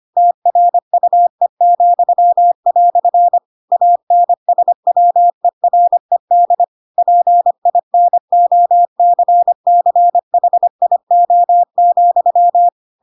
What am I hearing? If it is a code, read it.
Morse code, heard as a keyed tone, about 25 wpm, 700 Hz. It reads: TRUE," ANSWERED PINOCCHIO,